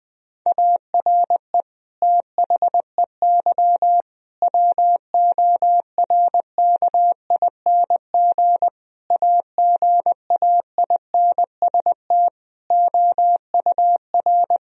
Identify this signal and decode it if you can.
Morse code, heard as a keyed tone, about 20 wpm, 700 Hz. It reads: ARE THEY WORKING AGAINST OUR
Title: Sentences from Top 1000 Words - Review Part 4 - 20wpm